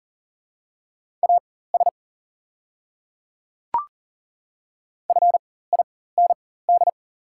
Morse code, heard as a keyed tone, about 39 wpm, spaced as for 19 wpm, 700 Hz. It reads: AS FIND
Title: 4x Character Spacing - Top 200 Words - 40wpm